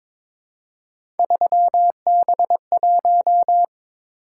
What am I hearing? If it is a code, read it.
Morse code, heard as a keyed tone, about 21 wpm, 700 Hz. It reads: 3B1
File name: practice / 22wpm / Three Letter-Number Combinations - 22wpm